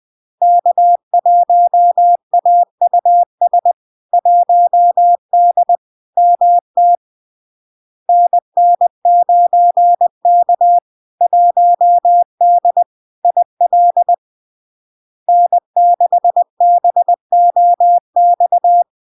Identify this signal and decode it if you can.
Morse code, heard as a keyed tone, about 20 wpm, 700 Hz. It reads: K1AUS 1D MT NN9K 1D IL N6BOX
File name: LARK-FD-2022-Log-in-CW-20wpm-700-Hz